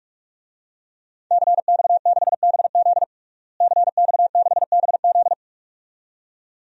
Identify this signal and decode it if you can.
Morse code, heard as a keyed tone, about 42 wpm, 700 Hz. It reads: CX6BZ CX6BZ
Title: DXCC-100 Call Signs - 5 Characters - Repeated 2x - 45wpm